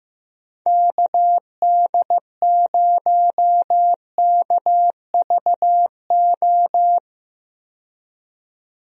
Morse code, heard as a keyed tone, about 15 wpm, 700 Hz. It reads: KD0KVO
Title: USA Call Signs - 6 Characters - 15wpm